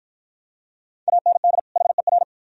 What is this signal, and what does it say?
Morse code, heard as a keyed tone, about 49 wpm, 700 Hz. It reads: AND HER